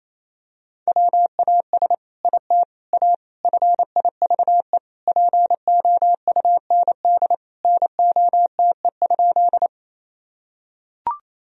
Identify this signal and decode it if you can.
Morse code, heard as a keyed tone, about 28 wpm, 700 Hz. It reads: WAS IT A FIVE POUND NOTE?
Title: Sentences from Top 1000 Words - Review Part 2 - 28wpm